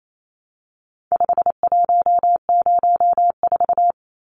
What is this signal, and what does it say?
Morse code, heard as a keyed tone, about 28 wpm, 700 Hz. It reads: 5104
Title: Numbers - 4-Digit 28wpm